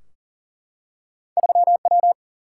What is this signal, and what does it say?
Morse code, heard as a keyed tone, about 40 wpm, 700 Hz. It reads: 3W
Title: Single-Letter Single-Number 40wpm